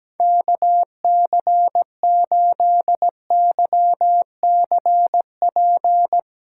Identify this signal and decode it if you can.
Morse code, heard as a keyed tone, about 17 wpm, 700 Hz. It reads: KC8YCP
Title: USA Call Signs - 6 Characters - 6x Character Spacing - 17wpm